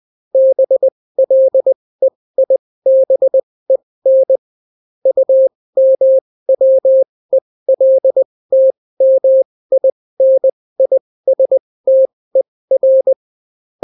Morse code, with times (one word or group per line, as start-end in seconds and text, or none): 0.3-4.4 BLEIBEN
5.1-13.1 UMWELTMINISTER